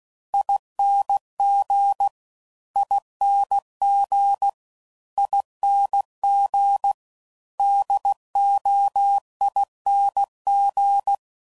Morse code, heard as a keyed tone, about 16 wpm, 800 Hz. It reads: ING ING ING DOING
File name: Suffixes / Special INGs / Sngl Ings